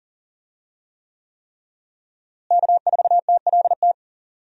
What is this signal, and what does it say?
Morse code, heard as a keyed tone, about 39 wpm, 700 Hz. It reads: K4TLT
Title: USA Call Signs - 5 Characters - Repeated 3x - 40wpm